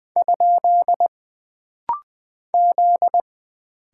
Morse code, heard as a keyed tone, about 20 wpm, 700 Hz. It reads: ? Z
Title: Single Character (Letter:Number:Punctuation:Prosign) - Rapid-Fire - 20wpm